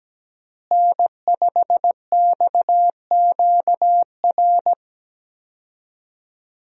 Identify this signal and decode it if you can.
Morse code, heard as a keyed tone, about 17 wpm, 700 Hz. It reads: N5XQR
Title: USA Call Signs - 5 Characters - Repeated 2x - 17wpm